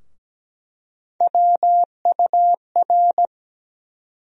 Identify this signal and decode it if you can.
Morse code, heard as a keyed tone, about 17 wpm, 700 Hz. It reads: WUR
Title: USA Call Sign Postfixes - 3 Letters 17wpm